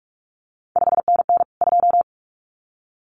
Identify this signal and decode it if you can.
Morse code, heard as a keyed tone, about 46 wpm, 700 Hz. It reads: 5NN 2
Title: CQ WW DX Contest Exchange Encore - 45wpm